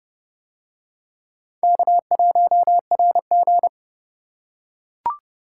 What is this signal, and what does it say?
Morse code, heard as a keyed tone, about 24 wpm, 700 Hz. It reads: K1RG
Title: Speed Racing - USA Call Signs 20wpm